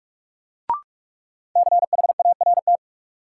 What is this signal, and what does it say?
Morse code, heard as a keyed tone, about 46 wpm, 700 Hz. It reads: CHART
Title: Sentences from Top 1300 Words - New Word Drill - 45wpm